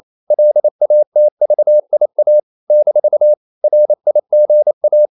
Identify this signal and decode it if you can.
Morse code, heard as a keyed tone, about 28 wpm, 600 Hz. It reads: LATVIA - RIGA
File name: CW / Practice / NationalCapitals28